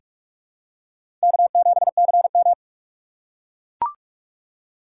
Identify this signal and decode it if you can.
Morse code, heard as a keyed tone, about 45 wpm, 700 Hz. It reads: K7CM
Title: USA Call Signs - 4 Characters - 4x Character Spacing - 45wpm